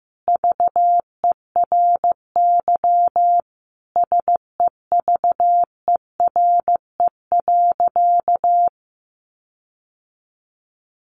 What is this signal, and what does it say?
Morse code, heard as a keyed tone, about 15 wpm, 700 Hz. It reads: VERY SEVERE.
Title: Sentences from Top 1400 Words 15wpm